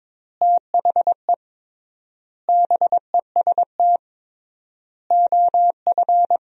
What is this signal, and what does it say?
Morse code, heard as a keyed tone, about 22 wpm, 700 Hz. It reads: THE BEST OF
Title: 3x Word-Spacing - Sentences from Top 400 Words - 22wpm